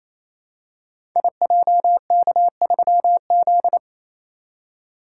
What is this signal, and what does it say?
Morse code, heard as a keyed tone, about 27 wpm, 700 Hz. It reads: IJK3Z
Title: Code Groups - 5 Letter:Numbers - 28wpm